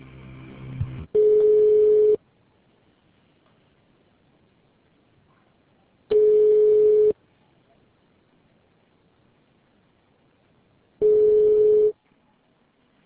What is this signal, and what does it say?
Ringback tone (Europe, 425 Hz).